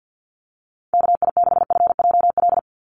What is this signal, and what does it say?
Morse code, heard as a keyed tone, about 50 wpm, 700 Hz. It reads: KI6FJL
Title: USA Call Signs 50wpm